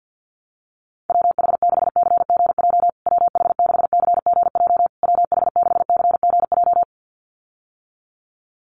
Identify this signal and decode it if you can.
Morse code, heard as a keyed tone, about 50 wpm, 700 Hz. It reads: WH6CGJ WH6CGJ WH6CGJ
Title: USA Call Signs Repeated 3x 50wpm